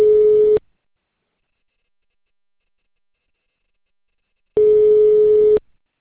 Ringback tone (Europe, 425 Hz).